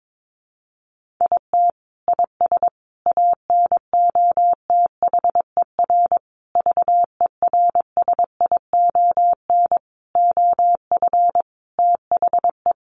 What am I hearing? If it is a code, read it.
Morse code, heard as a keyed tone, about 22 wpm, 700 Hz. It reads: IT IS ANOTHER VERSION OF THE
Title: Sentences from Top 1400 Words 22wpm